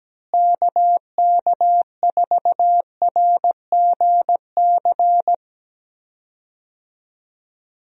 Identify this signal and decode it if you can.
Morse code, heard as a keyed tone, about 17 wpm, 700 Hz. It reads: KK4RGC
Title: USA Call Signs - 6 Characters - Repeated 3x - 17wpm